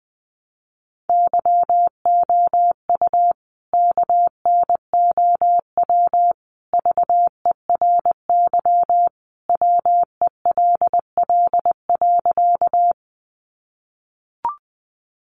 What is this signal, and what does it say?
Morse code, heard as a keyed tone, about 20 wpm, 700 Hz. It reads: YOU KNOW VERY WELL.
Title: A Sound Of Thunder 20wpm